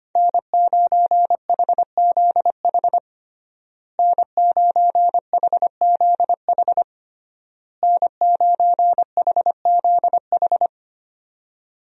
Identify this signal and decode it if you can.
Morse code, heard as a keyed tone, about 25 wpm, 700 Hz. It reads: N9HZH N9HZH N9HZH
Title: USA Call Signs - 5 Characters - Repeated 3x - 25wpm